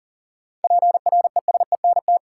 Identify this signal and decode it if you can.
Morse code, heard as a keyed tone, about 40 wpm, 700 Hz. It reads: PRESENT
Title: Sentences from Top 800 Words - New Word Drill - 40wpm